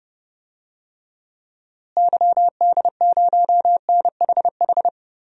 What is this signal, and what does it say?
Morse code, heard as a keyed tone, about 30 wpm, 700 Hz. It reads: YD0NHH
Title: DXCC-100 Call Signs - 6 Characters - Repeated 2x - 30wpm